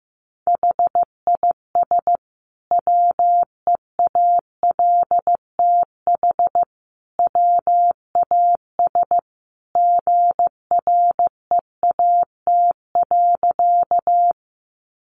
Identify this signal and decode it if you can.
Morse code, heard as a keyed tone, about 15 wpm, 700 Hz. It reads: HIS WEALTH WAS GREAT.
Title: Sentences from Top 1200 Words 15wpm